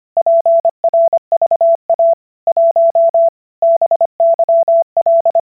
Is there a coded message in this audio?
Yes: Morse code from 0.2 to 5.4 s, heard as a keyed tone, about 25 wpm, 670 Hz. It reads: PRVA 1 BYL